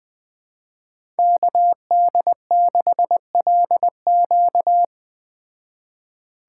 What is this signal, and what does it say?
Morse code, heard as a keyed tone, about 20 wpm, 700 Hz. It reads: KD6LQ
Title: Speed Racing - USA 2x2 Call Signs 17wpm